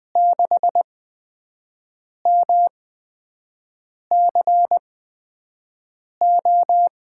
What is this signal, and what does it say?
Morse code, heard as a keyed tone, about 20 wpm, spaced as for 5 wpm, 700 Hz. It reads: 6MCO
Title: USA Call Signs - 5 Characters - 8x Character Spacing - 20wpm